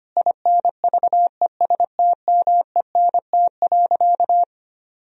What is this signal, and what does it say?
Morse code, heard as a keyed tone, about 25 wpm, 700 Hz. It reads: INVESTMENT.
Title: Sentences from Top 1000 Words - Review Part 4 - 25wpm